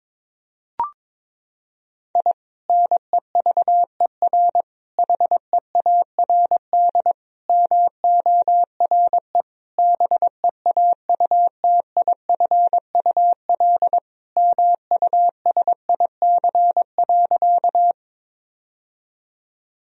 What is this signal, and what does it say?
Morse code, heard as a keyed tone, about 22 wpm, 700 Hz. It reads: I NEVER HEARD MORE BEAUTIFUL MUSIC.
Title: Sentences from Top 1000 Words - Review Part 1 - 22wpm